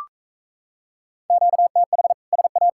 Morse code, heard as a keyed tone, about 42 wpm, 700 Hz. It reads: QTH SA